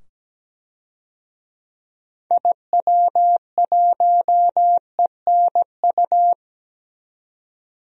Morse code, heard as a keyed tone, about 17 wpm, 700 Hz. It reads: IW1ENU
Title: DXCC-100 Call Signs - 6 Characters - 17wpm